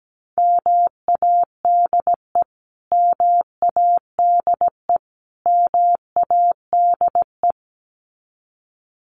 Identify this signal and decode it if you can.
Morse code, heard as a keyed tone, about 17 wpm, 700 Hz. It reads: MADE MADE MADE
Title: Top 500 Words Repeated 17wpm